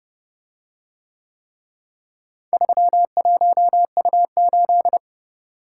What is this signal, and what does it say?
Morse code, heard as a keyed tone, about 28 wpm, 700 Hz. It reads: 31U8